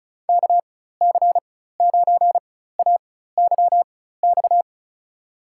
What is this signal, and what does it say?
Morse code, heard as a keyed tone, about 36 wpm, spaced as for 16 wpm, 690 Hz. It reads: KC9AYX
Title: USA Call Signs - 6 Characters - 4x Character Spacing - 35wpm